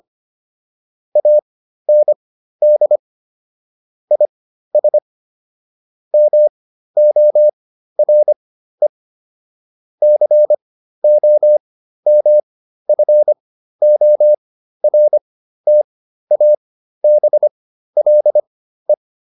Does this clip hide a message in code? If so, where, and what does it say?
Yes: Morse code from 1.2 to 18.9 s, heard as a keyed tone, about 25 wpm, spaced as for 13 wpm, 600 Hz. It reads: AND IS MORE COMFORTABLE